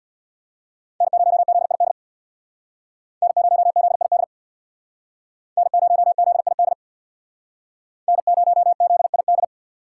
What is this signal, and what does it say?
Morse code, heard as a keyed tone, about 64 wpm, 700 Hz. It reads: N0ZID N0ZID N0ZID N0ZID